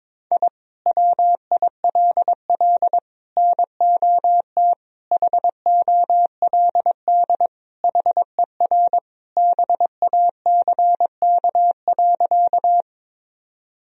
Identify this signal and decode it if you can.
Morse code, heard as a keyed tone, about 22 wpm, 700 Hz. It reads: I WILL NOT HOLD HER BACK.
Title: Sentences from Top 1000 Words - Review Part 2 - 22wpm